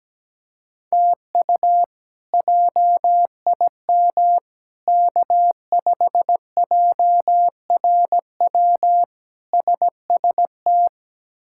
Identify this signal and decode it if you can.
Morse code, heard as a keyed tone, about 17 wpm, 700 Hz. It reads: TU JIM K5JRW SST
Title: K1USN SST QSO Practice - 17wpm